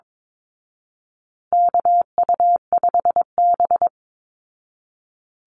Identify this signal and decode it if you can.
Morse code, heard as a keyed tone, about 22 wpm, 700 Hz. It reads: KU5B
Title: Call Signs Repeated 2x - 22wpm